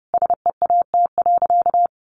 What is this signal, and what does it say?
Morse code, heard as a keyed tone, about 28 wpm, 700 Hz. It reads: SEAT.
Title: Sentences from Top 1500 Words 30wpm